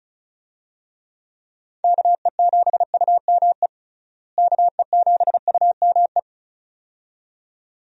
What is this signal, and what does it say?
Morse code, heard as a keyed tone, about 36 wpm, 700 Hz. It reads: KE7UME KE7UME
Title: USA Call Signs - 6 Characters - Repeated 2x - 35wpm